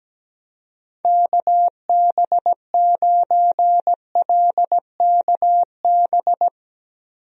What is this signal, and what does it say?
Morse code, heard as a keyed tone, about 17 wpm, 700 Hz. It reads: KB9LKB
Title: USA Call Signs - 6 Characters - 17wpm